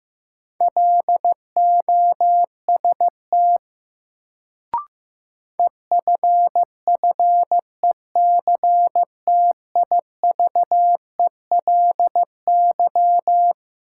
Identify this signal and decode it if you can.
Morse code, heard as a keyed tone, about 15 wpm, 700 Hz. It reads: LOST EFFECTIVELY
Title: Sentences from Top 1400 Words - New Word Drill Alt - 15wpm